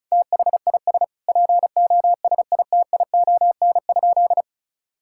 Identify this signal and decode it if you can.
Morse code, heard as a keyed tone, about 34 wpm, 700 Hz. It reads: THIS POSITION?
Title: Sentences from Top 1000 Words - Review Part 5 - 35wpm